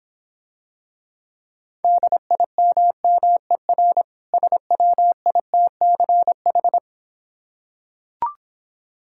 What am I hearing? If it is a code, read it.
Morse code, heard as a keyed tone, about 26 wpm, 690 Hz. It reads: DIMMER SWITCH